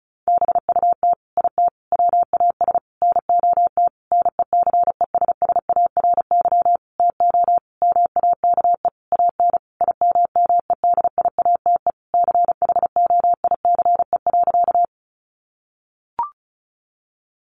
Morse code, heard as a keyed tone, about 34 wpm, 700 Hz. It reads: BUT IT WAS NOT NECESSARY TO MAKE AN IMMEDIATE CHOICE.
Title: Sentences from Top 1200 Words 35wpm